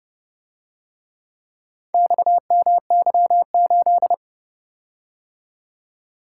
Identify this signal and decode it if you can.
Morse code, heard as a keyed tone, about 29 wpm, 700 Hz. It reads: XMY8